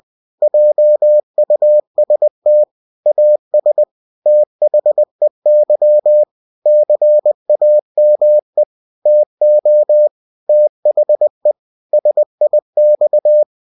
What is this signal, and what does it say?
Morse code, heard as a keyed tone, about 20 wpm, 600 Hz. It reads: JUST AS THEY CAME TO THE SIX
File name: cw-books / winnie-the-pooh / winnie-20wpm-page-0034